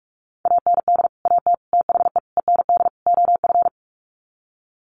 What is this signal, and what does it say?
Morse code, heard as a keyed tone, about 44 wpm, 700 Hz. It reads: AND AT THE END OF
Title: Sets of 5 Words 45wpm